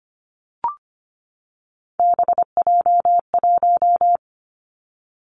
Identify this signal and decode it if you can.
Morse code, heard as a keyed tone, about 25 wpm, 700 Hz. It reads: BJ1
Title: Three Challenging Letter-Number Combinations 25wpm